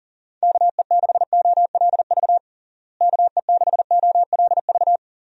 Morse code, heard as a keyed tone, about 40 wpm, 700 Hz. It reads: KE6OLV KE6OLV
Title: USA Call Signs - 6 Characters - Repeated 3x - 40wpm